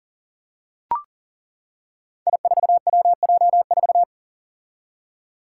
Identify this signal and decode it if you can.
Morse code, heard as a keyed tone, about 38 wpm, 700 Hz. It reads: I4WJ4
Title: Five Challenging Letter-Number Combinations 40wpm